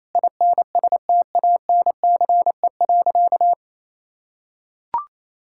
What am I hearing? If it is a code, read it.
Morse code, heard as a keyed tone, about 28 wpm, 700 Hz. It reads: INSTANCE.
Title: Sentences from Top 1000 Words - Review Part 5 - 28wpm